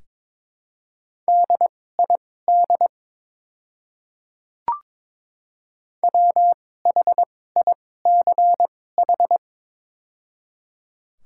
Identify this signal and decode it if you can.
Morse code, heard as a keyed tone, about 22 wpm, spaced as for 16 wpm, 700 Hz. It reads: DID WHICH
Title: 2x Character Spacing - Top 500 Words - 22wpm